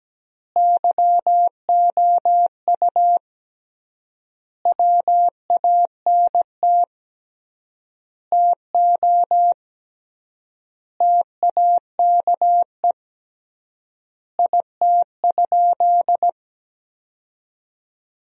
Morse code, heard as a keyed tone, about 17 wpm, 700 Hz. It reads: YOU WANT TO TAKE IT?
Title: 3x Word-Spacing - Sentences from Top 300 Words - 17wpm